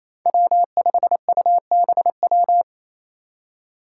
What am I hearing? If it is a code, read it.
Morse code, heard as a keyed tone, about 28 wpm, 700 Hz. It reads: W5UBW
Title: Five Challenging Letter-Number Combinations 28wpm